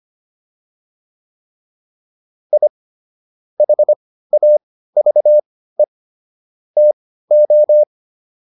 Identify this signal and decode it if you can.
Morse code, heard as a keyed tone, about 25 wpm, spaced as for 15 wpm, 600 Hz. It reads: I HAVE TO